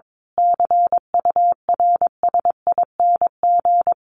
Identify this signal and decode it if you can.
Morse code, heard as a keyed tone, about 22 wpm, 700 Hz. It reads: CURSING
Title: A Sound Of Thunder 22wpm